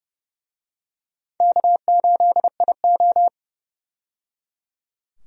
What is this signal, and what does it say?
Morse code, heard as a keyed tone, about 30 wpm, 700 Hz. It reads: K8IO
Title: USA Call Signs - 4 Characters - 30wpm